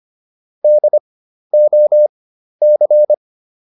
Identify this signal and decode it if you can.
Morse code, heard as a keyed tone, about 25 wpm, spaced as for 12 wpm, 600 Hz. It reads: DOC